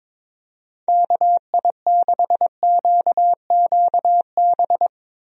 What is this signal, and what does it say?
Morse code, heard as a keyed tone, about 22 wpm, 700 Hz. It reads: KI6QQB
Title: USA Call Signs - 6 Characters - 22wpm